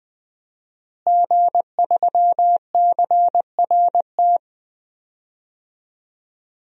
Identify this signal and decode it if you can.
Morse code, heard as a keyed tone, about 20 wpm, 700 Hz. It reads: G3CRT
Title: DXCC-100 Call Signs - 5 Characters - Repeated 2x - 20wpm